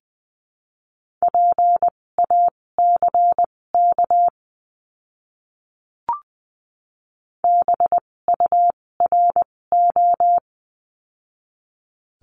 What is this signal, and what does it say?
Morse code, heard as a keyed tone, about 20 wpm, spaced as for 16 wpm, 700 Hz. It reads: PACK BURO
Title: 256 Lesson 42a - Words 12-20wpm